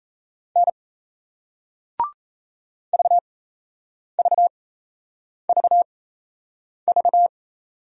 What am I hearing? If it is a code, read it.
Morse code, heard as a keyed tone, about 38 wpm, spaced as for 8 wpm, 700 Hz. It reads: N VVVV